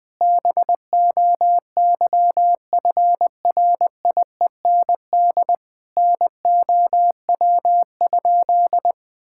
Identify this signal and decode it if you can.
Morse code, heard as a keyed tone, about 20 wpm, 700 Hz. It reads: BOYFRIEND NOW?
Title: Sentences from Top 1000 Words - Review Part 5 - 20wpm